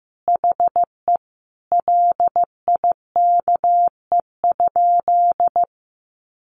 Morse code, heard as a keyed tone, about 15 wpm, 700 Hz. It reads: HE LIKE?
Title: Sentences from Top 100 Words 15wpm